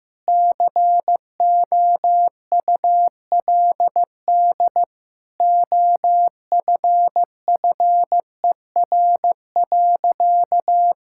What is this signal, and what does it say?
Morse code, heard as a keyed tone, about 15 wpm, 700 Hz. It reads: COULD OFFER.
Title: Sentences from Top 1600 Words 15wpm